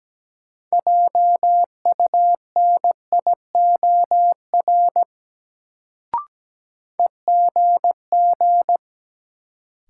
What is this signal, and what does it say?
Morse code, heard as a keyed tone, about 17 wpm, 700 Hz. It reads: JUNIOR EGG
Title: Sentences from Top 1000 Words - Review Part 7 - New Word Drill - 17wpm